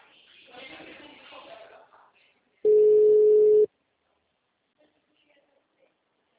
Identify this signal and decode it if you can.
Ringback tone (Europe, 425 Hz), with speech.